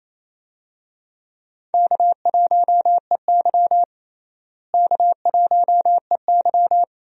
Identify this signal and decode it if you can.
Morse code, heard as a keyed tone, about 28 wpm, 700 Hz. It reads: K1EY K1EY
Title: USA Call Signs - 4 Characters - Repeated 2x - 28wpm